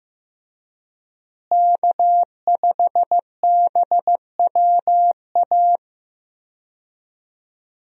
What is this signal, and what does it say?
Morse code, heard as a keyed tone, about 15 wpm, 700 Hz. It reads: K5BWA
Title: USA Call Signs - 5 Characters - Repeated 3x - 15wpm